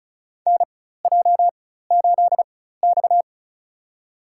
Morse code, heard as a keyed tone, about 34 wpm, spaced as for 16 wpm, 700 Hz. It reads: NJ8X
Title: USA Call Signs - 4 Characters - 4x Character Spacing - 35wpm